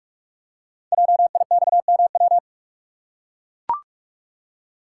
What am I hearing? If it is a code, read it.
Morse code, heard as a keyed tone, about 44 wpm, 700 Hz. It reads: JIXMW
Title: Code Groups - 5 Letters - 45wpm